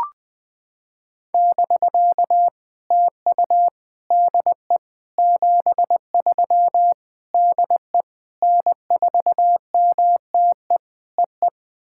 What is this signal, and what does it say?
Morse code, heard as a keyed tone, about 20 wpm, 700 Hz. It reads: <BK> TU DE 73 DE N4MTE EE
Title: POTA QSO Practice - 20wpm